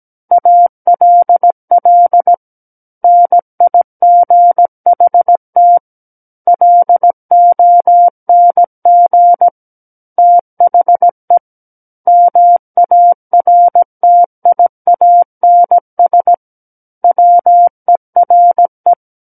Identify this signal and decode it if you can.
Morse code, heard as a keyed tone, about 17 wpm, 700 Hz. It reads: ALL NIGHT LONG THE MARTIANS WERE
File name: Book One / 08-Chapter 8 - 17 WPM